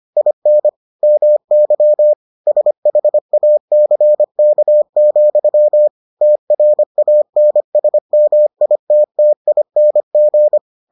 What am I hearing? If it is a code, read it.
Morse code, heard as a keyed tone, about 25 wpm, 600 Hz. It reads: IN MY SHACK, TRANSMITTING